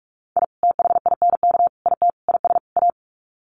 Morse code, heard as a keyed tone, about 44 wpm, 700 Hz. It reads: I THINK IT IS A
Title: Sets of 5 Words Encore 45wpm